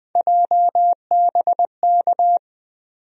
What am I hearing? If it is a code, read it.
Morse code, heard as a keyed tone, about 20 wpm, 700 Hz. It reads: JBK
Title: USA Call Sign Postfixes - 3 Letters 20wpm